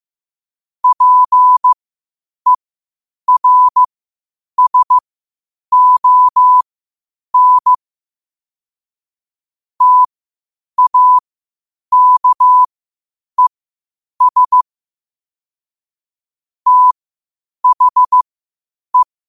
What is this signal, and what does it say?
Morse code, heard as a keyed tone, about 15 wpm, spaced as for 8 wpm, 1000 Hz. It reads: PERSON TAKES THE
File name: qotdfiles / 8wpm